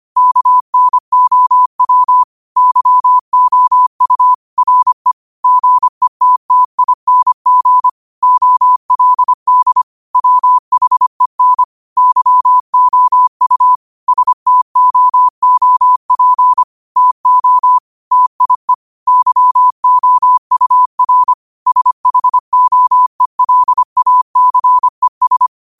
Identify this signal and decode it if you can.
Morse code, heard as a keyed tone, about 25 wpm, 1000 Hz. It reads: KNOW YOU RE GETTING OLD WHEN YOU STOOP TO TIE YOUR SHOELACES
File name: qotdfiles / 25wpm